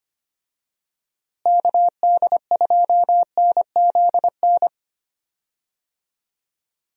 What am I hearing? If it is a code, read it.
Morse code, heard as a keyed tone, about 25 wpm, 700 Hz. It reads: KD2NZN
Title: USA Call Signs - 6 Characters - Repeated 2x - 25wpm